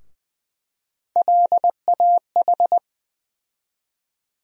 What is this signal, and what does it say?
Morse code, heard as a keyed tone, about 20 wpm, 700 Hz. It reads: LAH
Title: USA Call Sign Postfixes - 3 Letters 20wpm